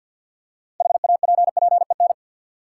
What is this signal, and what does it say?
Morse code, heard as a keyed tone, about 52 wpm, 700 Hz. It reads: HAPPEN
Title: Six Letter Words - 50wpm